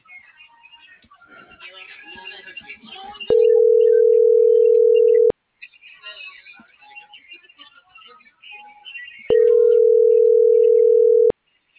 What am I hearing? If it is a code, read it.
Ringback tone (North America), 440 + 480 Hz.